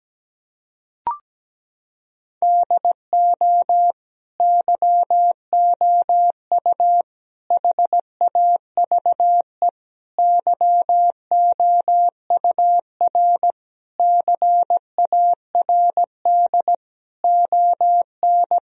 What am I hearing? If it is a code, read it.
Morse code, heard as a keyed tone, about 17 wpm, 700 Hz. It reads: DO YOU HAVE YOUR CARD ON
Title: Sentences from Top 1000 Words - Review Part 4 - 17wpm